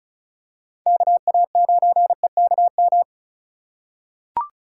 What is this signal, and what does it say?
Morse code, heard as a keyed tone, about 34 wpm, 700 Hz. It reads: KA9EKM
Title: USA Call Signs - 6 Characters - 2x Character Spacing - 35wpm